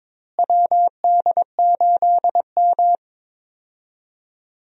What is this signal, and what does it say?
Morse code, heard as a keyed tone, about 22 wpm, 700 Hz. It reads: WD8M
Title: USA Call Signs - 4 Characters - Repeated 3x - 22wpm